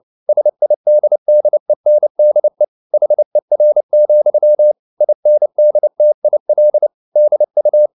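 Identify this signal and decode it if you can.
Morse code, heard as a keyed tone, about 29 wpm, 600 Hz. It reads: SIDDENDE HER, INDTIL DU
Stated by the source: Fast (QRQ) Morse at 29 wpm, from CW med Gnister.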